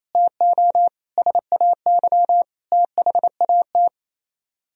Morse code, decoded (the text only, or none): TO SAY THAT